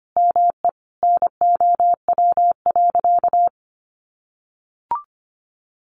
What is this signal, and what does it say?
Morse code, heard as a keyed tone, about 25 wpm, 700 Hz. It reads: ME NOW.
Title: Sentences from Top 200 Words 25wpm